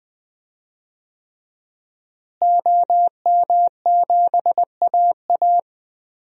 Morse code, heard as a keyed tone, about 21 wpm, 700 Hz. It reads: OM7AA
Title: DXCC-100 Call Signs - 5 Characters - Repeated 3x - 20wpm